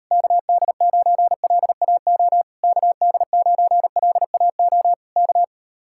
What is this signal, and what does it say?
Morse code, heard as a keyed tone, about 38 wpm, 700 Hz. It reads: KD9LAO KD9LAO K